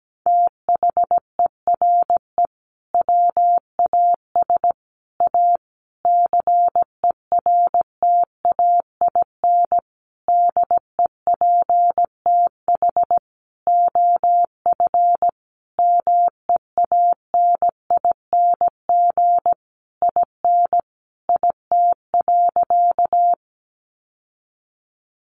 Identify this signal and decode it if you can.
Morse code, heard as a keyed tone, about 17 wpm, 700 Hz. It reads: THERE WAS A CERTAIN DEPTH OF MEANING IN IT.
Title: Sentences from Top 1200 Words 17wpm